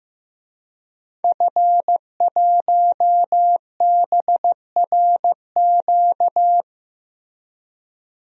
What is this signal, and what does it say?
Morse code, heard as a keyed tone, about 15 wpm, 700 Hz. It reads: F1BRQ
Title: DXCC-100 Call Signs - 5 Characters - Repeated 3x - 15wpm